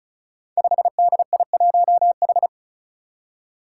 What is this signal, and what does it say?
Morse code, heard as a keyed tone, about 36 wpm, 700 Hz. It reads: 5DI1H
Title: Five Challenging Letter-Number Combinations 35wpm